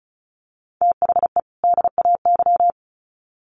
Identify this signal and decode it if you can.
Morse code, heard as a keyed tone, about 34 wpm, 700 Hz. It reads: THE DAY
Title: Sets of 2 Words 35wpm